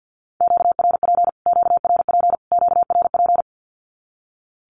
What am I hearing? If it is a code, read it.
Morse code, heard as a keyed tone, about 50 wpm, 700 Hz. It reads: QRP QRP QRP
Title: Q-Codes Repeated 3x 50wpm